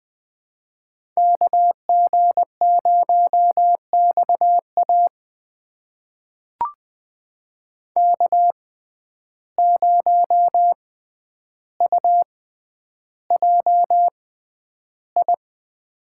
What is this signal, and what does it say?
Morse code, heard as a keyed tone, about 20 wpm, spaced as for 10 wpm, 700 Hz. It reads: KG0XA K0UJI
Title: USA Call Signs - 5 Characters - 6x Character Spacing - 20wpm